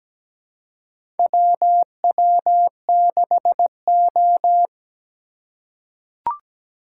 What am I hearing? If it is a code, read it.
Morse code, heard as a keyed tone, about 17 wpm, 700 Hz. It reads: WW6O
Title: USA Call Signs - 4 Characters - 2x Character Spacing - 17wpm